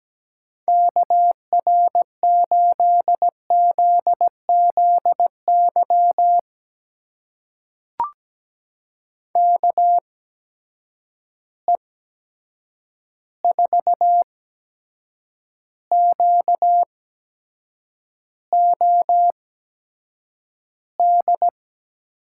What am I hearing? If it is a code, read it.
Morse code, heard as a keyed tone, about 17 wpm, spaced as for 7 wpm, 700 Hz. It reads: KR8ZZY KE4QOD
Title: USA Call Signs - 6 Characters - 8x Character Spacing - 17wpm